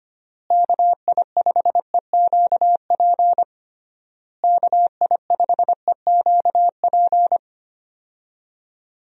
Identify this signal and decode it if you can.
Morse code, heard as a keyed tone, about 25 wpm, 700 Hz. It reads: KI5EQP KI5EQP
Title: USA Call Signs - 6 Characters - Repeated 3x - 25wpm